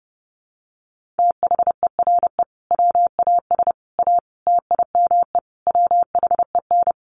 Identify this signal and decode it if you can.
Morse code, heard as a keyed tone, about 30 wpm, 700 Hz. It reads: THERE WAS A TIME WHEN
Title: Sets of 5 Words 30wpm